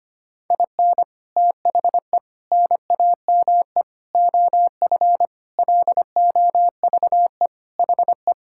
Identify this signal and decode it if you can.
Morse code, heard as a keyed tone, about 25 wpm, 700 Hz. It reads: IN THE NAME OF LOVE HE
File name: practice / 25wpm / Sentences with 4-Word Phrases - 25wpm